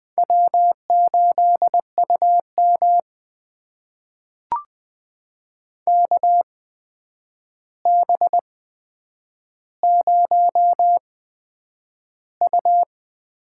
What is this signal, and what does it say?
Morse code, heard as a keyed tone, about 20 wpm, spaced as for 9 wpm, 700 Hz. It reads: W8UM KB0U
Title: USA Call Signs - 4 Characters - 8x Character Spacing - 20wpm